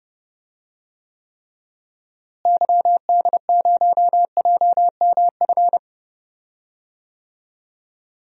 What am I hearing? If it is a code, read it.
Morse code, heard as a keyed tone, about 30 wpm, 700 Hz. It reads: YD0JMF
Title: DXCC-100 Call Signs - 6 Characters - Repeated 3x - 30wpm